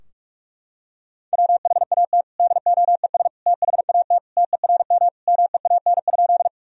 Morse code, heard as a keyed tone, about 44 wpm, 700 Hz. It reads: WHAT DOES THAT TERM MEAN?
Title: Sentences from Top 1000 Words - Review Part 4 - 45wpm